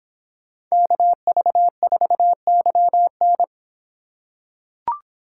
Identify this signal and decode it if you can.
Morse code, heard as a keyed tone, about 24 wpm, 700 Hz. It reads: KV4YN